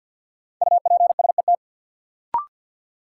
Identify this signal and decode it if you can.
Morse code, heard as a keyed tone, about 52 wpm, 700 Hz. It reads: UPSET
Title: Sentences from Top 800 Words - New Word Drill - 50wpm